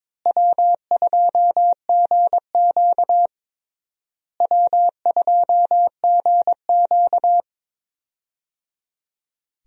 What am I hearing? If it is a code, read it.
Morse code, heard as a keyed tone, about 21 wpm, 700 Hz. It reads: W2GQ W2GQ
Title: USA Call Signs - 4 Characters - Repeated 3x - 22wpm